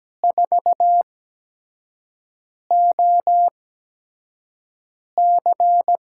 Morse code, heard as a keyed tone, about 17 wpm, spaced as for 5 wpm, 700 Hz. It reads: 4OC
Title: USA Call Signs - 6 Characters - 8x Character Spacing - 17wpm